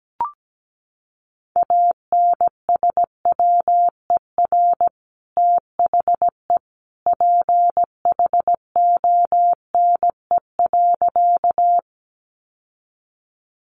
Morse code, heard as a keyed tone, about 17 wpm, 700 Hz. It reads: ANSWER THE PHONE.
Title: Sentences from Top 1100 Words 17wpm